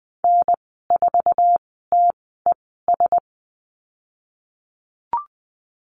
Morse code, heard as a keyed tone, about 20 wpm, spaced as for 15 wpm, 700 Hz. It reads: N4TES
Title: 048 Lesson 9g - Number 4 - Call Signs 20wpm